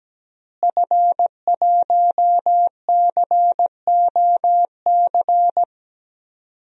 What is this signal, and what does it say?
Morse code, heard as a keyed tone, about 17 wpm, 700 Hz. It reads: F1COC
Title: DXCC-100 Call Signs - 5 Characters - Repeated 2x - 17wpm